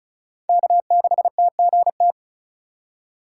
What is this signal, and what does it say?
Morse code, heard as a keyed tone, about 34 wpm, 700 Hz. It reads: K6TGT
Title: USA Call Signs - 3x Character Spacing - 35wpm